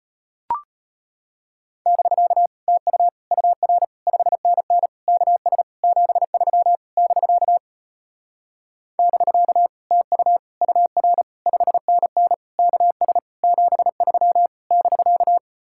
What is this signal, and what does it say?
Morse code, heard as a keyed tone, about 33 wpm, 700 Hz. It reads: <BK> TU UR 5NN KS 73 <BK> <BK> TU UR 5NN KS 73 <BK>